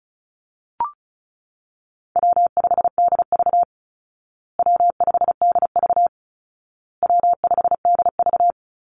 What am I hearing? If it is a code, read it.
Morse code, heard as a keyed tone, about 34 wpm, 700 Hz. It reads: W5DV W5DV W5DV
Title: Easy Call Signs Repeated 3x - 35wpm